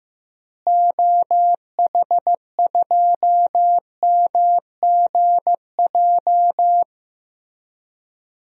Morse code, heard as a keyed tone, about 15 wpm, 700 Hz. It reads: OH2MGJ
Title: DXCC-100 Call Signs - 6 Characters - Repeated 3x - 15wpm